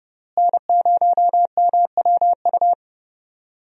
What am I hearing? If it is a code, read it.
Morse code, heard as a keyed tone, about 29 wpm, 700 Hz. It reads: N0MWU
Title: USA Call Signs - 5 Characters - Repeated 3x - 30wpm